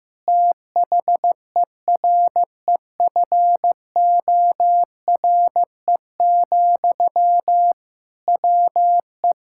Morse code, heard as a keyed tone, about 15 wpm, 700 Hz. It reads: THEREFORE, WE
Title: Sentences from Top 1000 Words - Review Part 5 - 15wpm